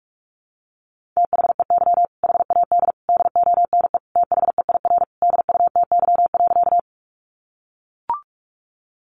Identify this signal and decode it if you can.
Morse code, heard as a keyed tone, about 44 wpm, 700 Hz. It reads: THEY HAD DONE THEIR DUTY.
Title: Sentences from Top 1500 Words 45wpm